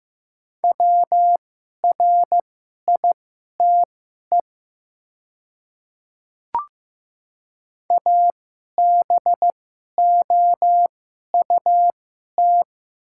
Morse code, heard as a keyed tone, about 15 wpm, spaced as for 11 wpm, 700 Hz. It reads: WRITE ABOUT
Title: 2x Character Spacing - Top 300 Words - 15wpm